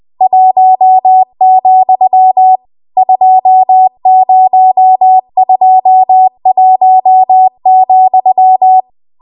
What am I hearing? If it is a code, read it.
Morse code, heard as a keyed tone, about 20 wpm, 750 Hz. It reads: 1, 2021,
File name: Archive / 20 WPM / 20WPM